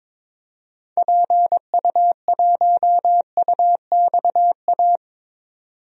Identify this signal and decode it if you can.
Morse code, heard as a keyed tone, about 22 wpm, 700 Hz. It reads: PU1UXA
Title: DXCC-100 Call Signs - 6 Characters - 22wpm